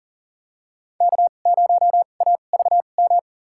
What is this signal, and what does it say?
Morse code, heard as a keyed tone, about 40 wpm, spaced as for 29 wpm, 700 Hz. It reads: K0AVM
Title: USA Call Signs - 5 Characters - 2x Character Spacing - 40wpm